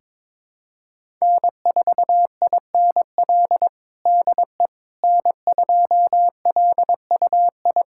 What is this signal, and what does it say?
Morse code, heard as a keyed tone, about 22 wpm, 700 Hz. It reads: N4INL DE N2LUI
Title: USA Callsign DE Callsign - 5 Characters - 22wpm